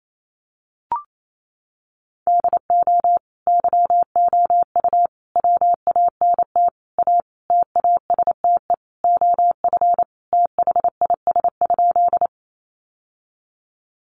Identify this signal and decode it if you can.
Morse code, heard as a keyed tone, about 28 wpm, 700 Hz. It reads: DO YOU WANT A TASTE OF THIS?
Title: Sentences from Top 1100 Words 28wpm